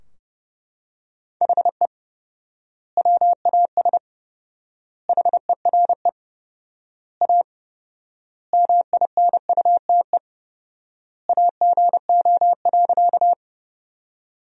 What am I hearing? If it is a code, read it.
Morse code, heard as a keyed tone, about 30 wpm, 700 Hz. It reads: HE WAS HERE A MINUTE AGO.
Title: 4x Word-Spacing - Sentences from Top 500 Words - 30wpm